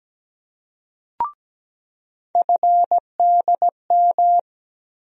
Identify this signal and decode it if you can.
Morse code, heard as a keyed tone, about 17 wpm, 700 Hz. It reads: FDM
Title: USA Call Sign Postfixes - 3 Letters 17wpm